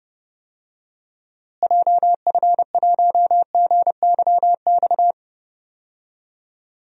Morse code, heard as a keyed tone, about 32 wpm, 700 Hz. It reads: JF1GYX
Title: DXCC-100 Call Signs - 6 Characters - Repeated 3x - 30wpm